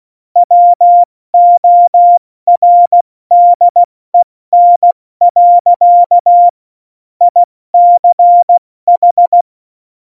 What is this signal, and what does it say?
Morse code, heard as a keyed tone, about 16 wpm, spaced as for 14 wpm, 700 Hz. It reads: WORDEN. ICH
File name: Morsekurs / Witze / Witz 05. Januar 2021, 70 F80